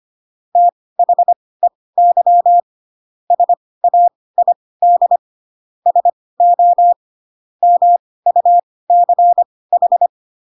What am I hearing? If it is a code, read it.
Morse code, heard as a keyed tone, about 25 wpm, spaced as for 18 wpm, 700 Hz. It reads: THEY SAID SO MUCH
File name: Pinocchio ch3 12 25 ik1hge